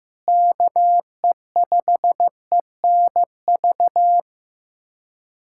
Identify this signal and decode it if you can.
Morse code, heard as a keyed tone, about 16 wpm, 700 Hz. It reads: KE5ENV
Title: USA Callsign DE Callsign - 6 Characters - 15wpm